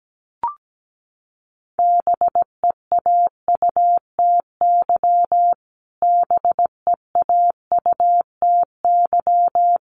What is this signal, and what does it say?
Morse code, heard as a keyed tone, about 17 wpm, 700 Hz. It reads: BEAUTY BEAUTY
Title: Top 500 Words Repeated 17wpm